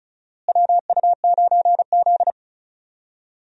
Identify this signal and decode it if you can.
Morse code, heard as a keyed tone, about 34 wpm, 700 Hz. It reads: WU9Z
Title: USA Call Signs - 4 Characters - Repeated 3x - 35wpm